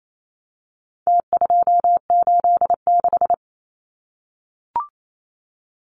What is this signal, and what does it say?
Morse code, heard as a keyed tone, about 28 wpm, 700 Hz. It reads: T286
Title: Numbers - Contest Style - 4-Digit 28wpm